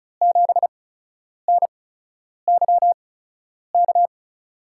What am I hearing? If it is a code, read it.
Morse code, heard as a keyed tone, about 36 wpm, spaced as for 10 wpm, 700 Hz. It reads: 7NYK
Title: USA Call Signs - 5 Characters - 8x Character Spacing - 35wpm